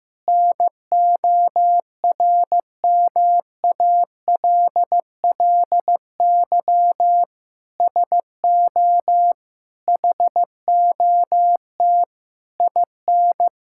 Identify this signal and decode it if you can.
Morse code, heard as a keyed tone, about 15 wpm, 700 Hz. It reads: NORMALLY SO HOT IN
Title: Sentences from Top 1000 Words - Review Part 6 - 15wpm